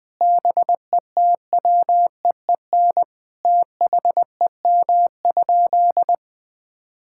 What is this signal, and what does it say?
Morse code, heard as a keyed tone, about 21 wpm, 700 Hz. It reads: BETWEEN THEM?
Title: Sentences from Top 1000 Words - Review Part 6 - 20wpm